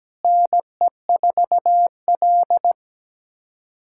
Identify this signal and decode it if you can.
Morse code, heard as a keyed tone, about 17 wpm, 700 Hz. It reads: NE4L
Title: USA Callsign DE Callsign - 4 Characters - 17wpm